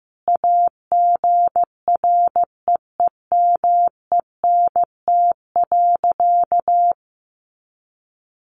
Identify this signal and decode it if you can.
Morse code, heard as a keyed tone, about 15 wpm, 700 Hz. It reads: AGREEMENT.
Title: Sentences from Top 1500 Words 15wpm